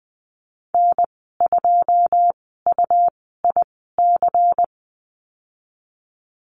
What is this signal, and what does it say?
Morse code, heard as a keyed tone, about 20 wpm, spaced as for 15 wpm, 700 Hz. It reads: N2UIC
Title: 128 Lesson 21g - Number 6 - Call Signs 20wpm